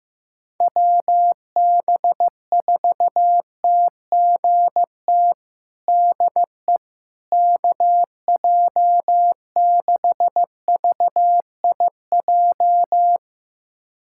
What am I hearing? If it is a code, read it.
Morse code, heard as a keyed tone, about 15 wpm, 700 Hz. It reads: WB4TGT DE KJ6VIJ
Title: USA Callsign DE Callsign - 6 Characters - 15wpm